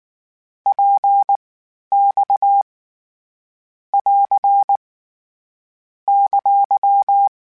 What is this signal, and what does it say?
Morse code, heard as a keyed tone, about 19 wpm, spaced as for 11 wpm, 800 Hz. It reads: PX <AR> !
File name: Short Runs / Short Runs-mp3s / Balanced @18